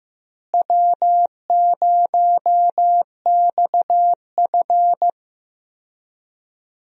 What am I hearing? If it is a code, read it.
Morse code, heard as a keyed tone, about 16 wpm, 700 Hz. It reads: W0XF
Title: USA Call Signs - 4 Characters - Repeated 3x - 15wpm